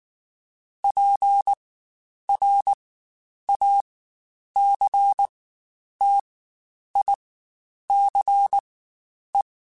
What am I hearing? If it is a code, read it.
Morse code, heard as a keyed tone, about 19 wpm, spaced as for 9 wpm, 800 Hz. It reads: PRACTICE
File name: Quotes Spaced / Quotes Spaced-mp3s / Deceive sp @18